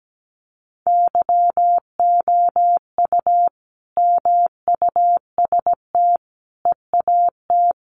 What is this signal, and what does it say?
Morse code, heard as a keyed tone, about 17 wpm, 700 Hz. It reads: YOU MUST EAT
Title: Sentences from Top 300 Words 17wpm